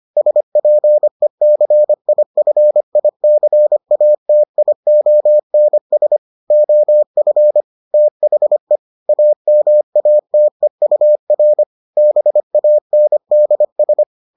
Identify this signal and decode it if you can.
Morse code, heard as a keyed tone, about 25 wpm, 600 Hz. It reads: SPECIFICATIONS OF THE AMATEUR BANDS